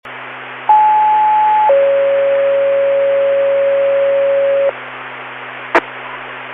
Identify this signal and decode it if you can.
Two-tone page, heard as a one-second tone, then a three-second tone, about 860 then 550 Hz, with noise. It is heard from 0.7 to 4.7 s.